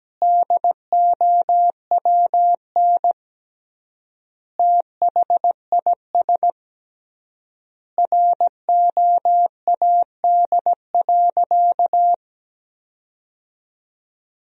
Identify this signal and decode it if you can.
Morse code, heard as a keyed tone, about 17 wpm, 700 Hz. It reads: DOWN THIS ROAD.
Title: 3x Word-Spacing - Sentences from Top 400 Words - 17wpm